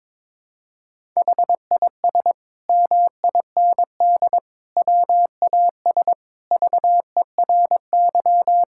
Morse code, heard as a keyed tone, about 21 wpm, 700 Hz. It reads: HIS MIND WAS VERY
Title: Sentences from Top 1000 Words - Review Part 6 - 22wpm